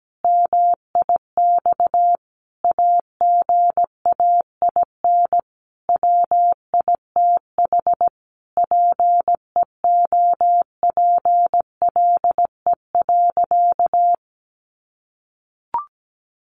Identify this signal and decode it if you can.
Morse code, heard as a keyed tone, about 17 wpm, 700 Hz. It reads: MIX AGAIN WITH PEOPLE.
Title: Sentences from Top 1100 Words 17wpm